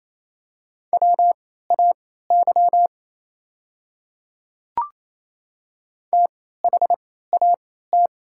Morse code, heard as a keyed tone, about 28 wpm, spaced as for 16 wpm, 700 Hz. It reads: WAY THAT
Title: 3x Character Spacing - Top 300 Words - 28wpm